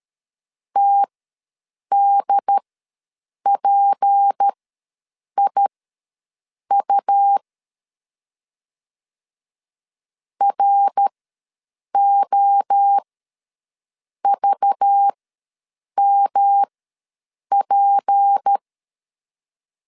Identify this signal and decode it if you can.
Morse code, heard as a keyed tone, about 13 wpm, spaced as for 7 wpm, 800 Hz. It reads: TDPIU ROVMP
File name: LEZIONI / 6 PKX0 / 6 P ws 12wpm 3117